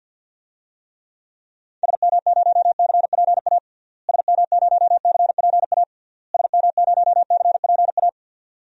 Morse code, heard as a keyed tone, about 52 wpm, 700 Hz. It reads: SM0CPA SM0CPA SM0CPA
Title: DXCC-100 Call Signs - 6 Characters - Repeated 3x - 50wpm